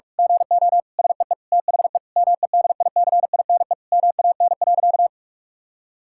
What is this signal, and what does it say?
Morse code, heard as a keyed tone, about 41 wpm, 700 Hz. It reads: GO SEE THE MEDICINE MAN.
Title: Sentences from Top 1000 Words - Review Part 6 - 45wpm